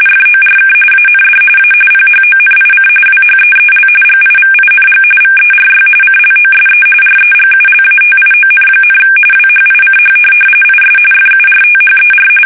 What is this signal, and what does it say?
RTTY, about 45.45 baud. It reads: FORK AND A PHONIC WHEEL GIVING VERY CLOSE REGULATION, WHICH IS FURTHER
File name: fsk850